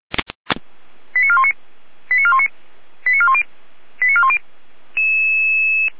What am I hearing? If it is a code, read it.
Five-tone selective call (ZVEI), heard as quick runs of five tones, tones of about 70 ms. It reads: 89419 89419 89410 89410